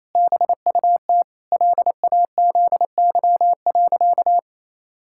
Morse code, heard as a keyed tone, about 28 wpm, 700 Hz. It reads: BUT LAZY.
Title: Pangram Challenge v2 - 28wpm